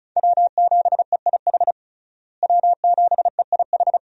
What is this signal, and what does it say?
Morse code, heard as a keyed tone, about 34 wpm, 700 Hz. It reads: W7EIH W7EIH
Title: USA Call Signs - 5 Characters - Repeated 2x - 35wpm